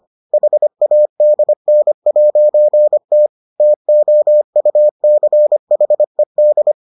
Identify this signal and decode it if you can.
Morse code, heard as a keyed tone, about 25 wpm, 600 Hz. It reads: HADN'T TOUCHED